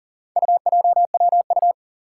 Morse code, heard as a keyed tone, about 40 wpm, 700 Hz. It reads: UJWU
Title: Four Challenging Letter-Number Combinations 40wpm